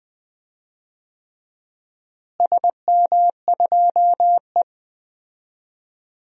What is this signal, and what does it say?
Morse code, heard as a keyed tone, about 21 wpm, 700 Hz. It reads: SM2E
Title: DXCC-100 Call Signs - 4 Characters - Repeated 3x - 20wpm